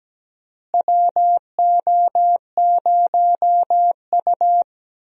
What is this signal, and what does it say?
Morse code, heard as a keyed tone, about 17 wpm, 700 Hz. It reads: WO0U
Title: USA Call Signs - 4 Characters - Repeated 3x - 17wpm